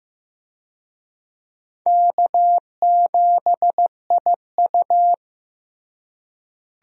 Morse code, heard as a keyed tone, about 15 wpm, 700 Hz. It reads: K7IU
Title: USA Call Signs - 4 Characters - Repeated 2x - 15wpm